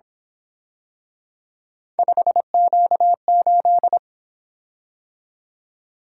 Morse code, heard as a keyed tone, about 26 wpm, 700 Hz. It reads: 5Q8
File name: Speed-Racing - Three Letter-Number Combinations - 22wpm